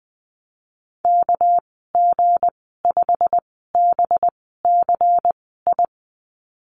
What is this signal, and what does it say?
Morse code, heard as a keyed tone, about 20 wpm, spaced as for 15 wpm, 700 Hz. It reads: KG5BCI
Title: 207 Lesson 33g - Letter j - Call Signs 20wpm